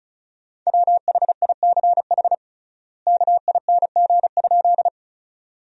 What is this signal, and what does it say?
Morse code, heard as a keyed tone, about 34 wpm, 700 Hz. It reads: WHICH KING?
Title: 3x Word-Spacing - Sentences from Top 400 Words - 35wpm